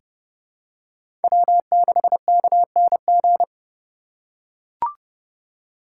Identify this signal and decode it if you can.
Morse code, heard as a keyed tone, about 30 wpm, 700 Hz. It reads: W6KNG
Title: USA Call Signs - 3x Character Spacing - 30wpm